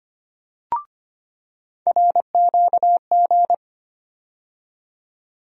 Morse code, heard as a keyed tone, about 25 wpm, 700 Hz. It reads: RQG
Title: USA Call Sign Postfixes - 3 Letters 25wpm